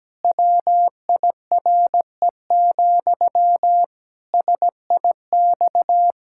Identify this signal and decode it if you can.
Morse code, heard as a keyed tone, about 17 wpm, 700 Hz. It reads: WIRE, SIX
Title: Pangram Challenge v2 - 17wpm